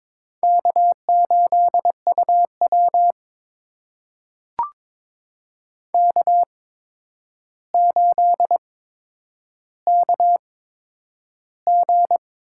Morse code, heard as a keyed tone, about 22 wpm, spaced as for 9 wpm, 700 Hz. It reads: K8UW K8KG
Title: USA Call Signs - 4 Characters - 8x Character Spacing - 22wpm